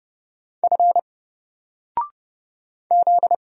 Morse code, heard as a keyed tone, about 31 wpm, 700 Hz. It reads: F Z
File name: Single-Letter-Number-30wpm